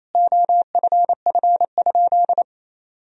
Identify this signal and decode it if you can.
Morse code, heard as a keyed tone, about 28 wpm, 700 Hz. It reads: OFF?
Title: Sentences from Top 500 Words and Abbreviations 28wpm